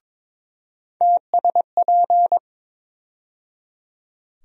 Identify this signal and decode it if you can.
Morse code, heard as a keyed tone, about 22 wpm, 700 Hz. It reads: TSP
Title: USA Call Sign Postfixes - 3 Letters 22wpm